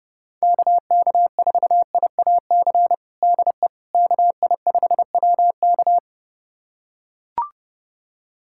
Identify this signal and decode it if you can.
Morse code, heard as a keyed tone, about 30 wpm, 700 Hz. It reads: KK4IAC DE KI5WK
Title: USA Callsign DE Callsign 30wpm